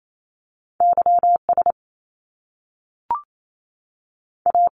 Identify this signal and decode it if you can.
Morse code, heard as a keyed tone, about 28 wpm, 700 Hz. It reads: Y S A
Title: Top Two Letter Combinations in Words 28wpm